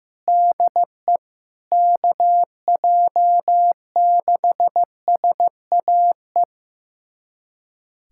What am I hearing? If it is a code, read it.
Morse code, heard as a keyed tone, about 16 wpm, 700 Hz. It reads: DE KJ6SAE
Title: USA Callsign DE Callsign 15wpm